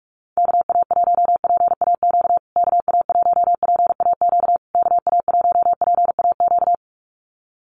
Morse code, heard as a keyed tone, about 44 wpm, 690 Hz. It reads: KA1PAQ KA1PAQ KA1PAQ
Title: USA Call Signs Repeated 3x 45wpm